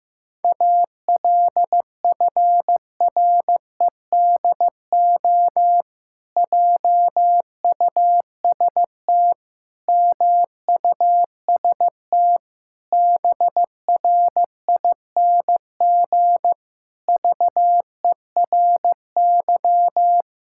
Morse code, heard as a keyed tone, about 15 wpm, 700 Hz. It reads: ALFREDO JUST MUST BRING VERY
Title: Pangram Challenge v2 - 15wpm